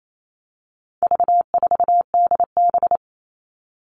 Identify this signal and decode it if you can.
Morse code, heard as a keyed tone, about 28 wpm, 700 Hz. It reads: V4DB
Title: All Cut Numbers - 4-Digit 28wpm